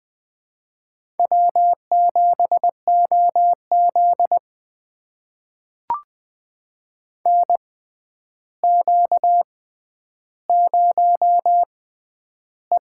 Morse code, heard as a keyed tone, about 21 wpm, spaced as for 10 wpm, 700 Hz. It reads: W7OZ NQ0E
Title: USA Call Signs - 4 Characters - 6x Character Spacing - 20wpm